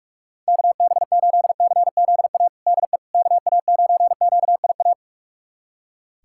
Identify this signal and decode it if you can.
Morse code, heard as a keyed tone, about 41 wpm, 700 Hz. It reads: KB8CZA DE KA9QIA
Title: USA Callsign DE Callsign 45wpm